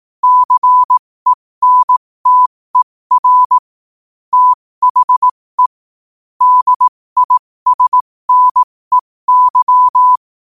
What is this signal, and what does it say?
Morse code, heard as a keyed tone, about 18 wpm, spaced as for 15 wpm, 1000 Hz. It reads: CENTER THE DISNEY
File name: 15wpm